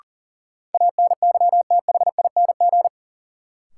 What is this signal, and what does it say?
Morse code, heard as a keyed tone, about 40 wpm, 700 Hz. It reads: ANYTHING
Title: Sets of 4 Words - New Word Drill - 40wpm